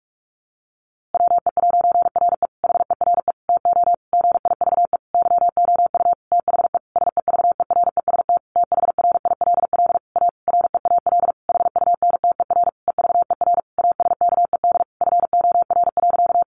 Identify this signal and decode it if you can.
Morse code, heard as a keyed tone, about 44 wpm, 700 Hz. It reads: WE'RE HERE TO GIVE YOU THE SEVEREST THRILL A REAL HUNTER EVER ASKED FOR.
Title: A Sound Of Thunder 45wpm